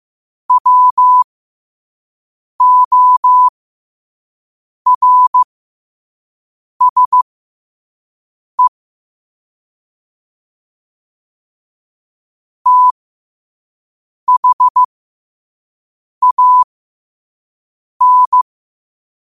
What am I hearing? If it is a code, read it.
Morse code, heard as a keyed tone, about 15 wpm, spaced as for 5 wpm, 1000 Hz. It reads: WORSE THAN
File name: 5wpm